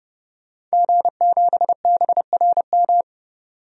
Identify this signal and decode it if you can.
Morse code, heard as a keyed tone, about 32 wpm, 700 Hz. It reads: G7BRM
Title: DXCC-100 Call Signs - 5 Characters - 30wpm